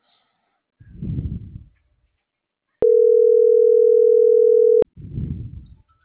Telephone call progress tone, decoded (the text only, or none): ringback tone